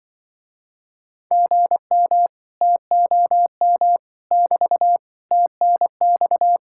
Morse code, heard as a keyed tone, about 24 wpm, 700 Hz. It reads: GM TOM <BT> TNX
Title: Speed-Racing Common QSO Phrases Encore - 20wpm